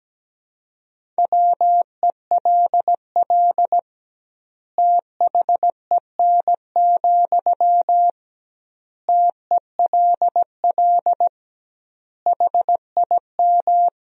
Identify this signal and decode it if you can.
Morse code, heard as a keyed tone, about 17 wpm, 700 Hz. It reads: WELL THEN, TELL HIM
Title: 2x Word-Spacing - Sentences from Top 200 Words - 17wpm